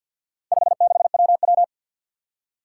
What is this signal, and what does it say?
Morse code, heard as a keyed tone, about 50 wpm, 700 Hz. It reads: 56WW
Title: Four Challenging Letter-Number Combinations 50wpm